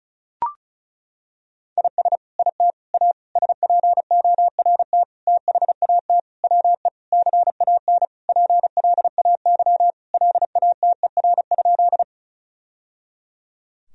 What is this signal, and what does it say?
Morse code, heard as a keyed tone, about 34 wpm, 700 Hz. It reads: IS IT A SPORT THAT WE CAN PLAY LATER?
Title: Sentences from Top 1000 Words - Review Part 4 - 35wpm